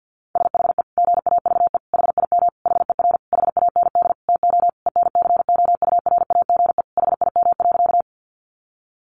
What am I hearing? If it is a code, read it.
Morse code, heard as a keyed tone, about 50 wpm, 700 Hz. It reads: SHE GAVE HIM HER HAND TO ENCOURAGE HIM.
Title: Sentences from Top 1400 Words 50wpm